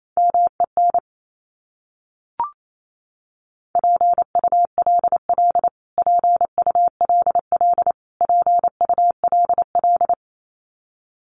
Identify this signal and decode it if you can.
Morse code, heard as a keyed tone, about 28 wpm, 700 Hz. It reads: MEN PULL PULL PULL
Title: Top 500 Words Repeated 28wpm